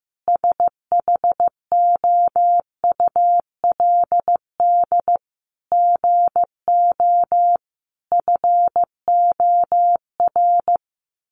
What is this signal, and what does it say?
Morse code, heard as a keyed tone, about 14 wpm, 700 Hz. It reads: SHOULD GO FOR
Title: Sentences from Top 1000 Words 15wpm